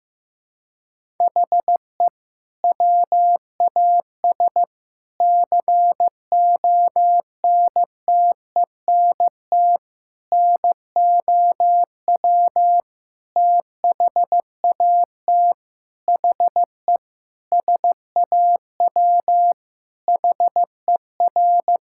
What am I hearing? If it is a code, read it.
Morse code, heard as a keyed tone, about 16 wpm, 700 Hz. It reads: HE WAS CONTENT NOW THAT HE SAW HER
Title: Sentences from Top 1000 Words - Review Part 6 - 15wpm